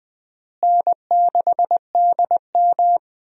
Morse code, heard as a keyed tone, about 20 wpm, 700 Hz. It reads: N6DM
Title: USA Call Signs - 4 Characters - Repeated 3x - 20wpm